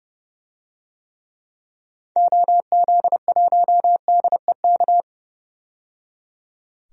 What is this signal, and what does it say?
Morse code, heard as a keyed tone, about 30 wpm, 700 Hz. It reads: OZ1DEK
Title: DXCC-100 Call Signs - 6 Characters - 30wpm